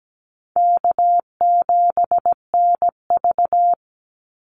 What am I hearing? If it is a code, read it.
Morse code, heard as a keyed tone, about 17 wpm, 700 Hz. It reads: K7NV
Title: Easy Call Signs Repeated 3x - 17wpm